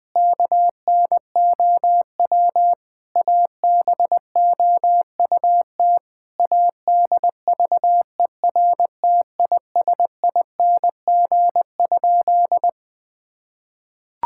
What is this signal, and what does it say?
Morse code, heard as a keyed tone, about 20 wpm, 700 Hz. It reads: KNOW ABOUT ADVERTISING?
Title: Sentences from Top 1000 Words - Review Part 6 - 20wpm